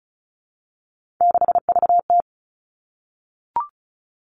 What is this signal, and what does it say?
Morse code, heard as a keyed tone, about 36 wpm, 700 Hz. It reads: 6VT
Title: Three Challenging Letter-Number Combinations 35wpm